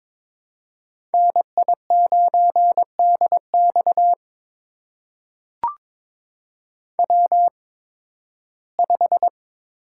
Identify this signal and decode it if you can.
Morse code, heard as a keyed tone, about 22 wpm, spaced as for 14 wpm, 700 Hz. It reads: NI9DX W5
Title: USA Call Signs - 5 Characters - 8x Character Spacing - 22wpm